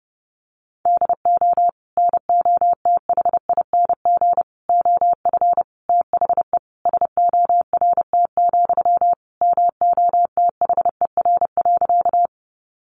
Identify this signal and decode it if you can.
Morse code, heard as a keyed tone, about 30 wpm, 700 Hz. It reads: DO NOTHING OF THE SORT, MOTHER.
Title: Sentences from Top 1300 Words 30wpm